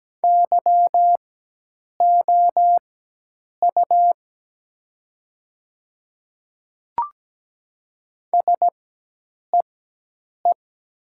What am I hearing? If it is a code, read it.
Morse code, heard as a keyed tone, about 17 wpm, spaced as for 8 wpm, 700 Hz. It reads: YOU SEE
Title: 4x Character Spacing - Top 100 Words - 17wpm